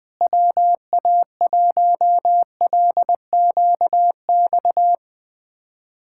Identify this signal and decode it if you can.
Morse code, heard as a keyed tone, about 20 wpm, 700 Hz. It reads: WA1LQX
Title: USA Callsign DE Callsign - 6 Characters - 20wpm